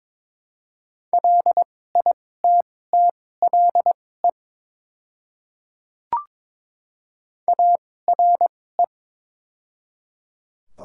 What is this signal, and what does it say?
Morse code, heard as a keyed tone, about 22 wpm, spaced as for 16 wpm, 700 Hz. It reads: LITTLE ARE
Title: 2x Character Spacing - Top 200 Words - 22wpm